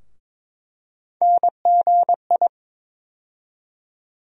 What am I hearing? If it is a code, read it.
Morse code, heard as a keyed tone, about 22 wpm, 700 Hz. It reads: NGI
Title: USA Call Sign Postfixes - 3 Letters 22wpm